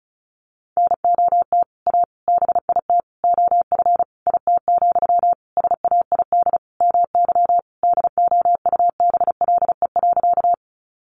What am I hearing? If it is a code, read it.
Morse code, heard as a keyed tone, about 34 wpm, 700 Hz. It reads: NOT A BIT OF IT, SAID MY DOUBLE.
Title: Sentences from Top 1400 Words 35wpm